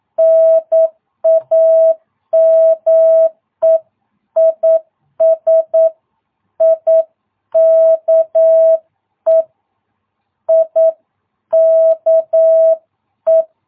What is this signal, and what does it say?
Morse code, heard as a keyed tone, about 9 wpm, 660 Hz. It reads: NAME IS IKE IKE
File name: jh8cbh / 2022cw